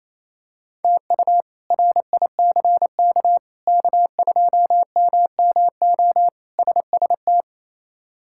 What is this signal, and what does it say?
Morse code, heard as a keyed tone, about 27 wpm, 700 Hz. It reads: TU RICK K2MMO SST
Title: K1USN SST QSO Practice - 28wpm